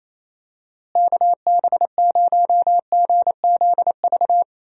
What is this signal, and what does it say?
Morse code, heard as a keyed tone, about 28 wpm, 700 Hz. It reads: KB0GZV
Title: USA Call Signs - 6 Characters - Repeated 3x - 28wpm